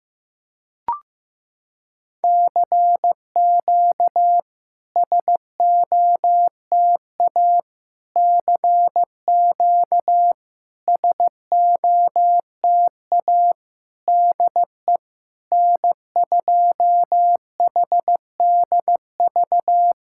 Morse code, heard as a keyed tone, about 15 wpm, 700 Hz. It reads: CQ SOTA CQ SOTA DE N2HDV
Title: SOTA QSO Practice 15wpm